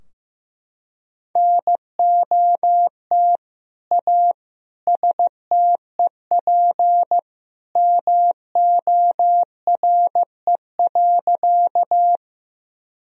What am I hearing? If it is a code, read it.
Morse code, heard as a keyed tone, about 15 wpm, 700 Hz. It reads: NOT A STEP MORE.
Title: Sentences from Top 1000 Words - Review Part 2 - 15wpm